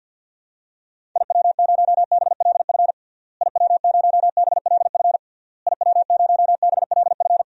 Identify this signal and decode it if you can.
Morse code, heard as a keyed tone, about 51 wpm, 700 Hz. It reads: IW0BLF IW0BLF IW0BLF
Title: DXCC-100 Call Signs - 6 Characters - Repeated 3x - 50wpm